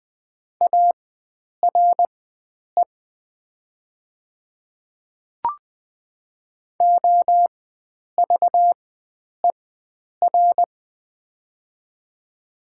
Morse code, heard as a keyed tone, about 20 wpm, spaced as for 9 wpm, 700 Hz. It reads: ARE OVER
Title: 4x Character Spacing - Top 100 Words - 20wpm